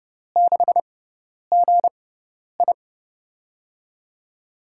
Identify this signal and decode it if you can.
Morse code, heard as a keyed tone, about 30 wpm, spaced as for 10 wpm, 700 Hz. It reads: 6GI
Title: USA Call Signs - 5 Characters - 6x Character Spacing - 30wpm